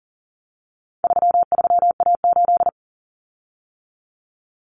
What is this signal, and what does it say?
Morse code, heard as a keyed tone, about 40 wpm, 700 Hz. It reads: 33A8
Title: Common Cut Numbers - 4-Digit 40wpm